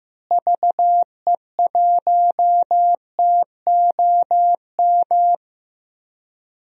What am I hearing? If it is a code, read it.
Morse code, heard as a keyed tone, about 15 wpm, 700 Hz. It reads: VE1TOM
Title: DXCC-100 Call Signs - 6 Characters - Repeated 3x - 15wpm